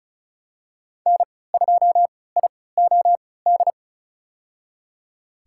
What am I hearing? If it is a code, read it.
Morse code, heard as a keyed tone, about 34 wpm, spaced as for 20 wpm, 700 Hz. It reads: N2IOD
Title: USA Call Signs - 5 Characters - 3x Character Spacing - 35wpm